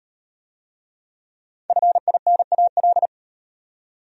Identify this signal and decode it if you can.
Morse code, heard as a keyed tone, about 35 wpm, 700 Hz. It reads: FINAL